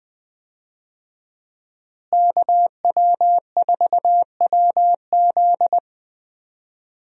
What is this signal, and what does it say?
Morse code, heard as a keyed tone, about 20 wpm, 700 Hz. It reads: KW4WZ
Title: USA Call Signs - 5 Characters - Repeated 2x - 20wpm